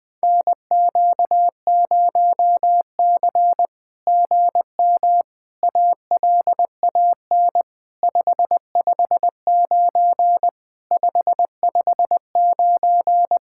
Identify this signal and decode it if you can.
Morse code, heard as a keyed tone, about 18 wpm, 700 Hz. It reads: NQ0C GM ALAN 559 559
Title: SOTA QSO Practice 20wpm